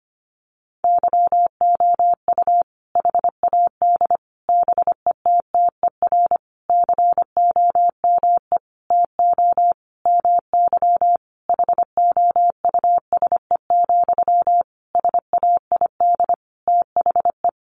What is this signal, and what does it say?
Morse code, heard as a keyed tone, about 25 wpm, 700 Hz. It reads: YOU HAD BETTER COME TO MY HOUSE, SAID THE
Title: Sentences from Top 400 Words 25wpm